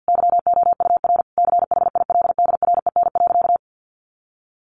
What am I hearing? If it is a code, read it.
Morse code, heard as a keyed tone, about 50 wpm, 700 Hz. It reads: YOUR CHILDREN.
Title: Sentences from Top 1200 Words 50wpm